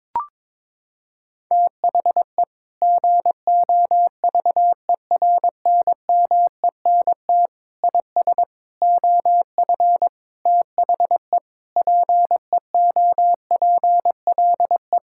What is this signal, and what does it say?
Morse code, heard as a keyed tone, about 22 wpm, 700 Hz. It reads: THE GOVERNMENT IS OF THE PEOPLE
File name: Sentences with 9-Word Phrases - 22wpm